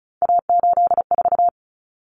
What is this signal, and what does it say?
Morse code, heard as a keyed tone, about 34 wpm, 700 Hz. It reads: A84
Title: Common Cut Numbers - 3-Digit 35wpm